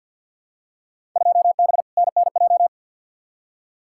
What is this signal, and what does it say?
Morse code, heard as a keyed tone, about 48 wpm, 700 Hz. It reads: 2B NNJ
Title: Field Day Exchanges Encore - 50wpm